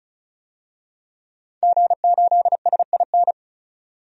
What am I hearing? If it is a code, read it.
Morse code, heard as a keyed tone, about 36 wpm, 700 Hz. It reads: G8SIN
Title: DXCC-100 Call Signs - 5 Characters - 35wpm